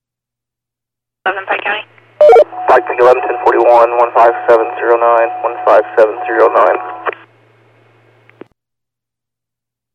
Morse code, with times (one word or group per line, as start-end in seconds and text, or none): none